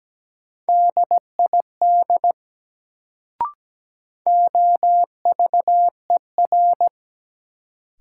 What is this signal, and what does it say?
Morse code, heard as a keyed tone, about 17 wpm, 700 Hz. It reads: DID OVER
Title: Sentences from Top 100 Words - New Word Drill - 17wpm